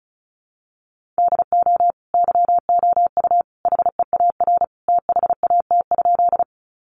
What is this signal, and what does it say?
Morse code, heard as a keyed tone, about 34 wpm, 700 Hz. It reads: DO YOU HEAR THAT?
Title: Sentences from Top 300 Words 35wpm